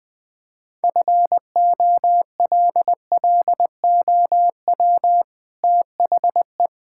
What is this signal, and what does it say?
Morse code, heard as a keyed tone, about 20 wpm, 700 Hz. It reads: FOLLOW THE
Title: Sentences from Top 1000 Words - Review Part 5 - 20wpm